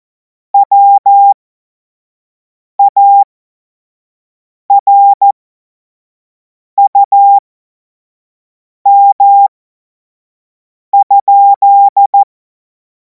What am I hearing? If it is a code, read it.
Morse code, heard as a keyed tone, about 14 wpm, spaced as for 5 wpm, 800 Hz. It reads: WARUM?
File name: Morsekurs / Witze / Witz 19. Mai 2026 25 F70